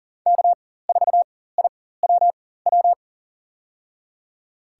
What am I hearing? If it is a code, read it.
Morse code, heard as a keyed tone, about 40 wpm, spaced as for 19 wpm, 700 Hz. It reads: K4IWW
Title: USA Call Signs - 5 Characters - 4x Character Spacing - 40wpm